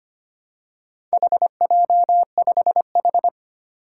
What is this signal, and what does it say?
Morse code, heard as a keyed tone, about 25 wpm, 700 Hz. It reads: HJ5H
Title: Four Challenging Letter-Number Combinations 25wpm